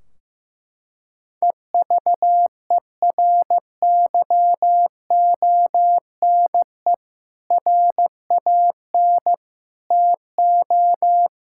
Morse code, heard as a keyed tone, about 15 wpm, 700 Hz. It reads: EVERYONE RAN TO
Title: Sentences from Top 1600 Words 15wpm